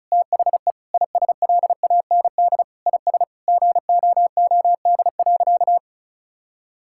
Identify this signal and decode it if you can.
Morse code, heard as a keyed tone, about 36 wpm, 700 Hz. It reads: THE ISLAND IS GOOD.
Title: Sentences from Top 1000 Words - Review Part 6 - 35wpm